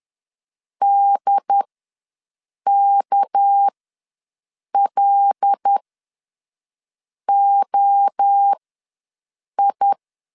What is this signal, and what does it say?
Morse code, heard as a keyed tone, about 10 wpm, spaced as for 5 wpm, 800 Hz. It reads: DKLOI
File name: LEZIONI / 7 GF9Z / 7 GF9Z ws 10wpm 3919